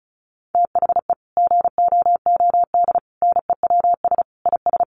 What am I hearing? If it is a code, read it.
Morse code, heard as a keyed tone, about 34 wpm, 700 Hz. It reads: THE GOOD NEWS IS
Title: Sets of 5 Words 35wpm